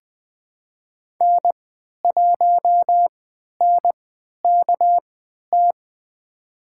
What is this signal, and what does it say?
Morse code, heard as a keyed tone, about 20 wpm, spaced as for 11 wpm, 700 Hz. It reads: N1NKT
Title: USA Call Signs - 5 Characters - 3x Character Spacing - 20wpm